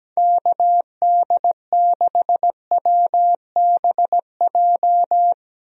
Morse code, heard as a keyed tone, about 17 wpm, 700 Hz. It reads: KD6WBJ
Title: USA Call Signs - 6 Characters - Repeated 3x - 17wpm